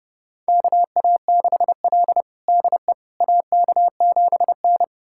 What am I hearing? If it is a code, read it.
Morse code, heard as a keyed tone, about 29 wpm, 700 Hz. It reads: KA6L DE AK7N
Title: USA Callsign DE Callsign - 4 Characters - 30wpm